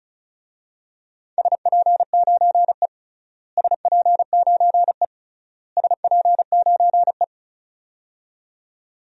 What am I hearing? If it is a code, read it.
Morse code, heard as a keyed tone, about 34 wpm, 700 Hz. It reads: SP9E SP9E SP9E
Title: DXCC-100 Call Signs - 4 Characters - Repeated 3x - 35wpm